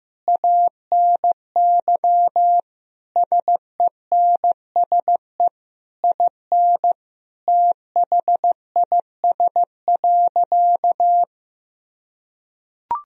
Morse code, heard as a keyed tone, about 15 wpm, 700 Hz. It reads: ANY SENSE IN THIS.
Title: Sentences from Top 1000 Words - Review Part 4 - 15wpm